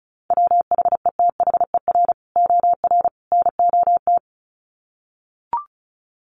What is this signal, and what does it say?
Morse code, heard as a keyed tone, about 34 wpm, 700 Hz. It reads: WHETHER OR NOT
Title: Sets of 3 Words 35wpm